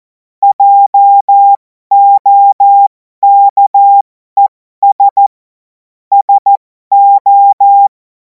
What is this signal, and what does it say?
Morse code, heard as a keyed tone, about 14 wpm, spaced as for 12 wpm, 800 Hz. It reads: JOKES SO
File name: Morsekurs / Witze